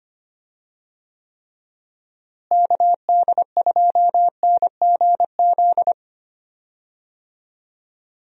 Morse code, heard as a keyed tone, about 25 wpm, 700 Hz. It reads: KD2NGZ
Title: USA Call Signs - 6 Characters - Repeated 2x - 25wpm